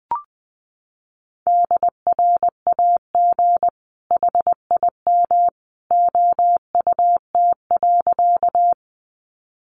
Morse code, heard as a keyed tone, about 20 wpm, 700 Hz. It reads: DRAG HIM OUT.
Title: Sentences from Top 1500 Words 20wpm